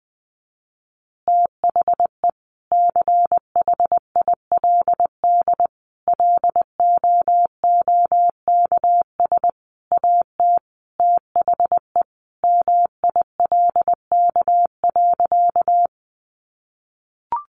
Morse code, heard as a keyed tone, about 20 wpm, 700 Hz. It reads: THE CHILD LOOKS AT THE MILK.
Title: Additional Easy Phrases 20wpm